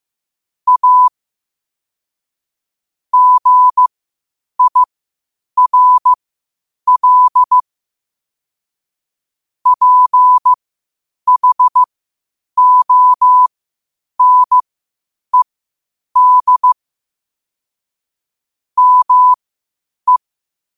Morse code, heard as a keyed tone, about 15 wpm, spaced as for 8 wpm, 1000 Hz. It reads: A GIRL PHONED ME